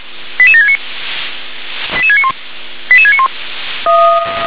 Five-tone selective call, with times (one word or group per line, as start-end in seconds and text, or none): 0.4-0.8 88769
2.0-2.3 88691
2.9-3.3 88691